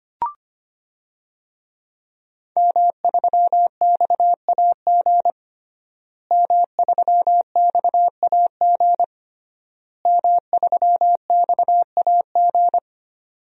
Morse code, heard as a keyed tone, about 25 wpm, 700 Hz. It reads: M3XAG M3XAG M3XAG
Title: DXCC-100 Call Signs - 5 Characters - Repeated 3x - 25wpm